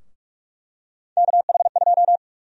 Morse code, heard as a keyed tone, about 46 wpm, 700 Hz. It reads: KH2
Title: USA Call Sign Prefixes - 45wpm